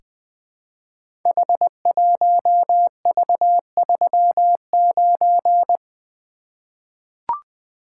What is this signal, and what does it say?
Morse code, heard as a keyed tone, about 20 wpm, 700 Hz. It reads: H1V39
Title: Code Groups - 5 Letter:Numbers - 20wpm